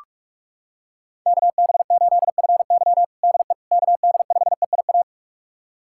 Morse code, heard as a keyed tone, about 46 wpm, 700 Hz. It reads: KB8FY DE KD5EIA
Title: USA Callsign DE Callsign 45wpm